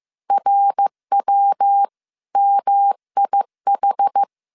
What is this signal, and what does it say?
Morse code, heard as a keyed tone, about 15 wpm, 800 Hz. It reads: RWMIH